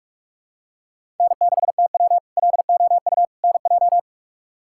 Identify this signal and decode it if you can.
Morse code, heard as a keyed tone, about 46 wpm, 700 Hz. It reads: N6TW LOU NJ
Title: K1USN SST QSO Practice - 45wpm